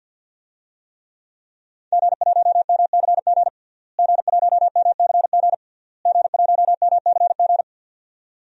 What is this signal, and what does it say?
Morse code, heard as a keyed tone, about 48 wpm, 700 Hz. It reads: G1MCG G1MCG G1MCG
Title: DXCC-100 Call Signs - 5 Characters - Repeated 3x - 50wpm